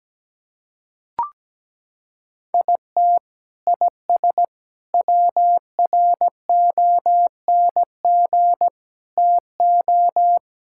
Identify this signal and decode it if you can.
Morse code, heard as a keyed tone, about 17 wpm, 700 Hz. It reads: IT IS WRONG TO
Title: Sentences from Top 1000 Words - Review Part 5 - 17wpm